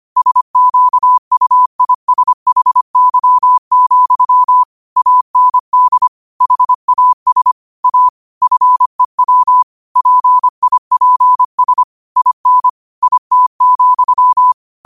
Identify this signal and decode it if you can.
Morse code, heard as a keyed tone, about 25 wpm, 1000 Hz. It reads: SQUISHY, AND HAS A FEW PIPS IN IT,
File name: qotdfiles / 25wpm